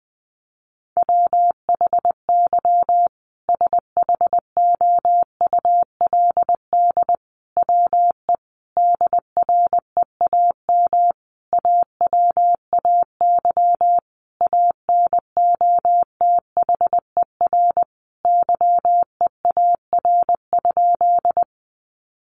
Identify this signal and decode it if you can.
Morse code, heard as a keyed tone, about 20 wpm, 700 Hz. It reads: WHY SHOULD WE DREAM AWAY ANOTHER YEAR?
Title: Sentences from Top 1500 Words 20wpm